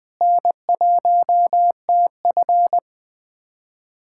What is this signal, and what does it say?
Morse code, heard as a keyed tone, about 20 wpm, 700 Hz. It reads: N1TF
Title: USA Callsign DE Callsign - 4 Characters - 20wpm